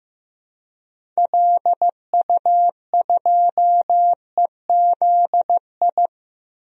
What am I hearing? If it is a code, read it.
Morse code, heard as a keyed tone, about 15 wpm, 700 Hz. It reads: LU2EZI
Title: DXCC-100 Call Signs - 6 Characters - Repeated 2x - 15wpm